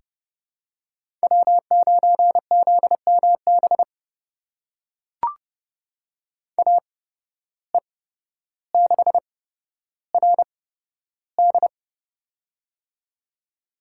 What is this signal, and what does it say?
Morse code, heard as a keyed tone, about 32 wpm, spaced as for 13 wpm, 700 Hz. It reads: W9ZMB AE6RD
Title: USA Call Signs - 5 Characters - 8x Character Spacing - 30wpm